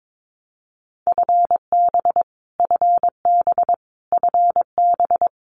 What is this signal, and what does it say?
Morse code, heard as a keyed tone, about 22 wpm, 700 Hz. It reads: FB FB FB
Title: Abbreviations Repeated 3x - 22wpm